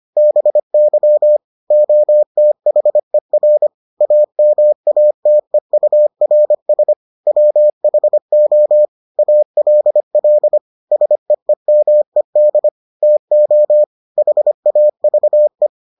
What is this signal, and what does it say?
Morse code, heard as a keyed tone, about 25 wpm, 600 Hz. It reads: BY OTHER AMATEURS WHO ALL SEEMED TO HAVE